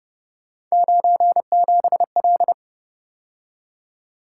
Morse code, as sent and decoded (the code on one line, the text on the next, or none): ----. --... .-..
97L